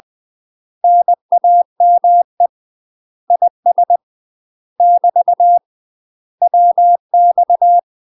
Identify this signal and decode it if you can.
Morse code, heard as a keyed tone, about 20 wpm, 700 Hz. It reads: NAME IS <BT> WX
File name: UK MASTER QSOs TRAINING / 20WPM / UK MASTER QSOs 0022